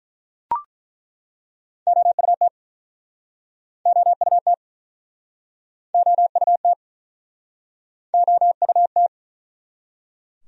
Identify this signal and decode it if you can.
Morse code, heard as a keyed tone, about 50 wpm, 700 Hz. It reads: OUT OUT OUT OUT